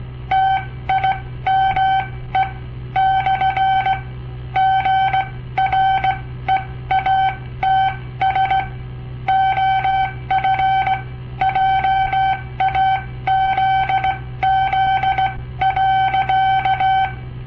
Morse code, heard as a keyed tone, about 17 wpm, spaced as for 14 wpm, 770 Hz. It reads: TIME / GREATS OF JAZZ.
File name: papertapes / local / Code Practice-01